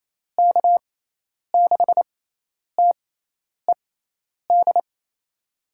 Morse code, heard as a keyed tone, about 28 wpm, spaced as for 10 wpm, 700 Hz. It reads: K6TED
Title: USA Call Signs - 6 Characters - 6x Character Spacing - 28wpm